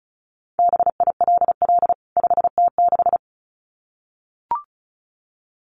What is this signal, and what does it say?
Morse code, heard as a keyed tone, about 34 wpm, 700 Hz. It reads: BILL 5T6
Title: CWT Exchange 35wpm